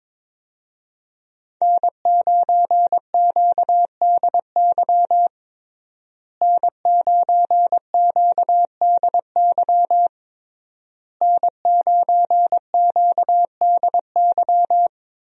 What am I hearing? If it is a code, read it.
Morse code, heard as a keyed tone, about 22 wpm, 700 Hz. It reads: N9QDY N9QDY N9QDY
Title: USA Call Signs - 5 Characters - Repeated 3x - 22wpm